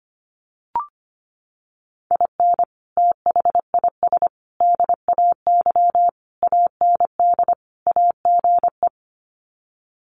Morse code, heard as a keyed tone, about 25 wpm, 700 Hz. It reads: IN THIS DAY AND AGE
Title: Sets of 5 Words Encore 25wpm